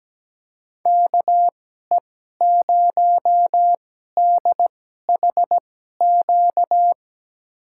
Morse code, heard as a keyed tone, about 17 wpm, spaced as for 12 wpm, 700 Hz. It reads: KE0DHQ
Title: USA Call Signs - 2x Character Spacing - 17wpm